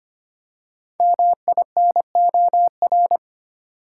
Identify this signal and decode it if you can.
Morse code, heard as a keyed tone, about 25 wpm, 700 Hz. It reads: MINOR
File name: 5 Letter Words - First 3 Letters Given - 25wpm